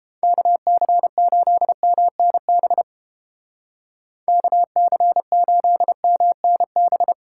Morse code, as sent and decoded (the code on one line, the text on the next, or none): -.- -.-. ---.. -- -. -... / -.- -.-. ---.. -- -. -...
KC8MNB KC8MNB